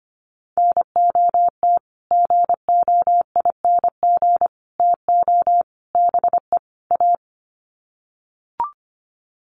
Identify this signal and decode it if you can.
Morse code, heard as a keyed tone, about 25 wpm, 700 Hz. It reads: NOT GOING TO BE A
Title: Sets of 5 Words Encore 25wpm